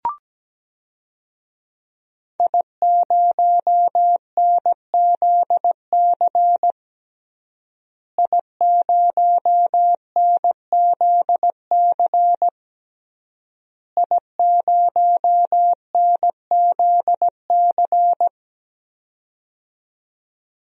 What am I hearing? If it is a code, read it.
Morse code, heard as a keyed tone, about 17 wpm, 700 Hz. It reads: I0NZC I0NZC I0NZC
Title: DXCC-100 Call Signs - 5 Characters - Repeated 3x - 17wpm